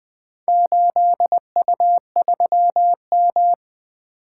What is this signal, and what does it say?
Morse code, heard as a keyed tone, about 20 wpm, 700 Hz. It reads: 8U3M
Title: Four Letter-Number Combinations - 20wpm